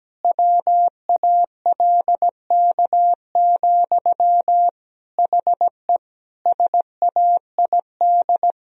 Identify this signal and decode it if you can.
Morse code, heard as a keyed tone, about 17 wpm, 700 Hz. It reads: WALK, HE SAID
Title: Sentences from Top 1000 Words - Review Part 2 - 17wpm